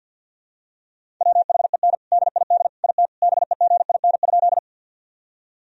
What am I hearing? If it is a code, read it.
Morse code, heard as a keyed tone, about 51 wpm, 700 Hz. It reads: WHEN DID IT BEGIN?
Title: Sentences from Top 1000 Words - Review Part 1 - 50wpm